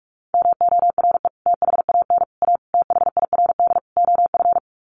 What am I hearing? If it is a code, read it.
Morse code, heard as a keyed tone, about 44 wpm, 700 Hz. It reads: MORE THAN A THIRD OF
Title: Sets of 5 Words Encore2 45wpm